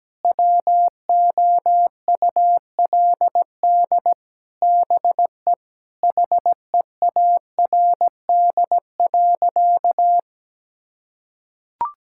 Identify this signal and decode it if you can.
Morse code, heard as a keyed tone, about 17 wpm, 700 Hz. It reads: WOULD BE HEARD.
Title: Sentences from Top 1600 Words 17wpm